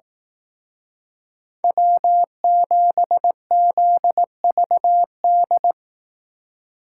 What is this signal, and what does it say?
Morse code, heard as a keyed tone, about 18 wpm, 700 Hz. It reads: W7ZVD
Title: Speed Racing - USA 1x3 Call Signs 15wpm